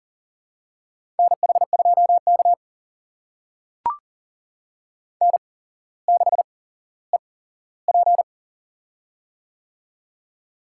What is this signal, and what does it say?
Morse code, heard as a keyed tone, about 40 wpm, spaced as for 17 wpm, 700 Hz. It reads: NH2K N6EP
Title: USA Call Signs - 4 Characters - 8x Character Spacing - 40wpm